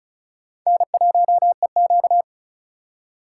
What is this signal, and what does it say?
Morse code, heard as a keyed tone, about 36 wpm, 700 Hz. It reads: N1EQ
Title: USA Call Signs - 4 Characters - 4x Character Spacing - 35wpm